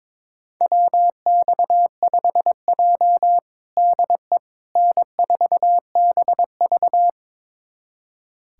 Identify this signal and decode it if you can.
Morse code, heard as a keyed tone, about 22 wpm, 700 Hz. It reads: WX5J DE N4BV
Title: USA Callsign DE Callsign - 4 Characters - 22wpm